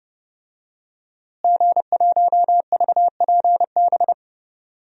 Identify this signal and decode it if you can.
Morse code, heard as a keyed tone, about 30 wpm, 700 Hz. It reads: G1VPB
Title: DXCC-100 Call Signs - 5 Characters - Repeated 3x - 30wpm